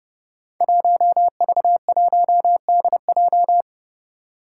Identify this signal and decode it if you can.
Morse code, heard as a keyed tone, about 30 wpm, 700 Hz. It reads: 1V1DJ
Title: Five Challenging Letter-Number Combinations 30wpm